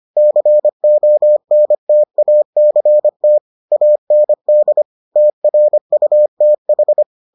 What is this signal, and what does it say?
Morse code, heard as a keyed tone, about 25 wpm, 600 Hz. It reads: CONTACT AND TRUTH